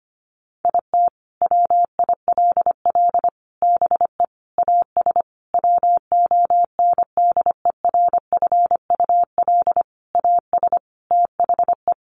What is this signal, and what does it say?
Morse code, heard as a keyed tone, about 25 wpm, 700 Hz. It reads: IT WILL BE AS WONDERFUL AS THE
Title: Sentences from Top 1400 Words 25wpm